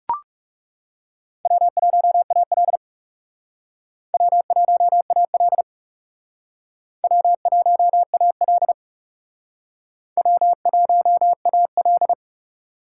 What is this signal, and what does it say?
Morse code, heard as a keyed tone, about 37 wpm, 700 Hz. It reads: W1AL W1AL W1AL W1AL
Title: Speed Racing - USA Call Signs 30wpm